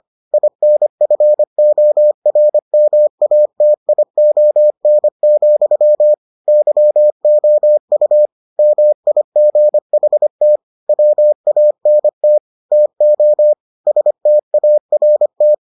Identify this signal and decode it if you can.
Morse code, heard as a keyed tone, about 25 wpm, 600 Hz. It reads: INFORMATION, YOU MIGHT WANT TO START